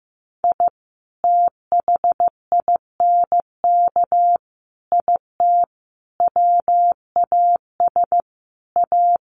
Morse code, heard as a keyed tone, about 15 wpm, 700 Hz. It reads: I THINK IT WAS A
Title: Sets of 5 Words 15wpm